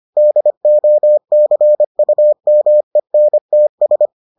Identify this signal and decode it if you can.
Morse code, heard as a keyed tone, about 25 wpm, 600 Hz. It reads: DOCUMENTS